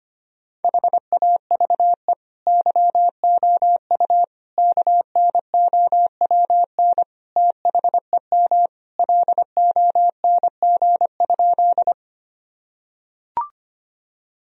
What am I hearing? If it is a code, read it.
Morse code, heard as a keyed tone, about 25 wpm, 700 Hz. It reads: HAVE YOU KNOWN THEM LONG?
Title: Sentences from Top 1000 Words - Review Part 3 - 25wpm